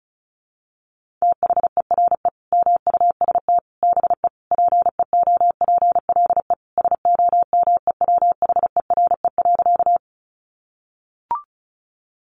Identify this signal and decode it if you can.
Morse code, heard as a keyed tone, about 34 wpm, 700 Hz. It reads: THERE MUST BE PEOPLE SOMEWHERE.
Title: Sentences from Top 1400 Words 35wpm